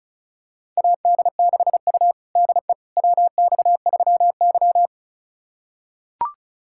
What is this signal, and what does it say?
Morse code, heard as a keyed tone, about 36 wpm, 700 Hz. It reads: AD6U DE WX3Y
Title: USA Callsign DE Callsign - 4 Characters - 35wpm